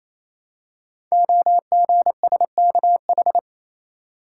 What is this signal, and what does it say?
Morse code, heard as a keyed tone, about 28 wpm, 700 Hz. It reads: OGSKH
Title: Code Groups - 5 Letters - 28wpm